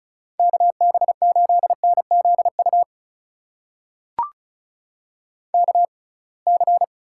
Morse code, heard as a keyed tone, about 34 wpm, spaced as for 26 wpm, 700 Hz. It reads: KB8NZU KC
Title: USA Call Signs - 6 Characters - 6x Character Spacing - 35wpm